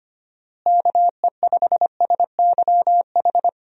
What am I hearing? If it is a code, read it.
Morse code, heard as a keyed tone, about 25 wpm, 700 Hz. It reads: KE5SYH
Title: USA Call Signs - 6 Characters - Repeated 2x - 25wpm